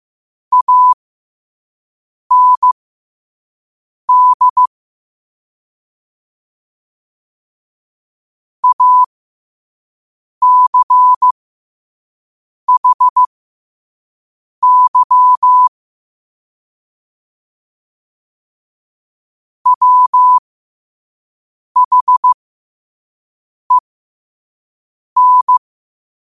Morse code, heard as a keyed tone, about 15 wpm, spaced as for 5 wpm, 1000 Hz. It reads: AND ACHY WHEN
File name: qotdfiles / 5wpm